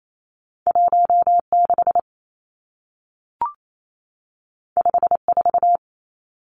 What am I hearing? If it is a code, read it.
Morse code, heard as a keyed tone, about 28 wpm, 700 Hz. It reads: 16 54
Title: Numbers - 2-Digit 28wpm